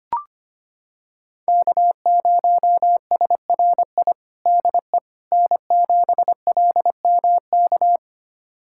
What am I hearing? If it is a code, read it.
Morse code, heard as a keyed tone, about 25 wpm, 700 Hz. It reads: K0SRI DE N7LMK
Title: USA Callsign DE Callsign - 5 Characters - 25wpm